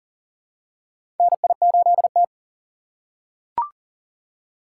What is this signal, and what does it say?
Morse code, heard as a keyed tone, about 40 wpm, 700 Hz. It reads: NI8T
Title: USA Call Signs - 4 Characters - 2x Character Spacing - 40wpm